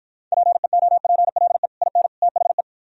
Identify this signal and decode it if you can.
Morse code, heard as a keyed tone, about 52 wpm, 700 Hz. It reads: PEOPLE IN THE